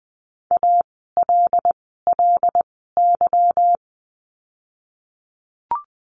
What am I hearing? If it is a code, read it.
Morse code, heard as a keyed tone, about 20 wpm, spaced as for 15 wpm, 700 Hz. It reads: ALLY
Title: 146 Lesson 24f - Letter y - Words 20wpm